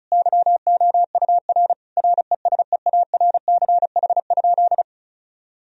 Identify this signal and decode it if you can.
Morse code, heard as a keyed tone, about 34 wpm, 700 Hz. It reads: YOUR RESEARCH?
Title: Sentences from Top 1000 Words - Review Part 4 - 35wpm